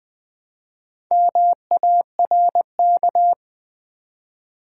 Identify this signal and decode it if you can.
Morse code, heard as a keyed tone, about 20 wpm, 700 Hz. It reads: MARK
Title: Build A Word - 1 Letter At A Time - 4-Letter Words - 20wpm